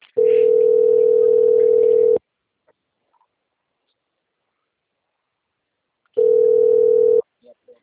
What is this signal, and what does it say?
Ringback tone (North America), 440 + 480 Hz.
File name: out- - - - - .14431